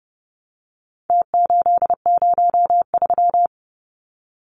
Morse code, heard as a keyed tone, about 30 wpm, 700 Hz. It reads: T803
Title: Numbers - Contest Style - 4-Digit 30wpm